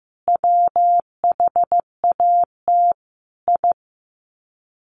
Morse code, heard as a keyed tone, about 15 wpm, 700 Hz. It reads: WHAT I
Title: Sets of 5 Words 15wpm